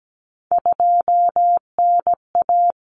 Morse code, heard as a keyed tone, about 17 wpm, 700 Hz. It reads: 2NA
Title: Common Cut Numbers - 3-Digit 17wpm